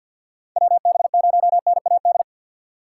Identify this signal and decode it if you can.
Morse code, heard as a keyed tone, about 45 wpm, 700 Hz. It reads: WB0NAD
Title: USA Call Signs - 6 Characters - 3x Character Spacing - 50wpm